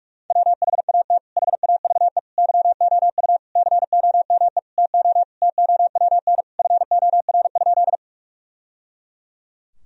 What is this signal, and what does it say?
Morse code, heard as a keyed tone, about 46 wpm, 700 Hz. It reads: WHAT HAVE YOU COME TO TOWN FOR?
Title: Sentences from Top 1000 Words - Review Part 2 - 45wpm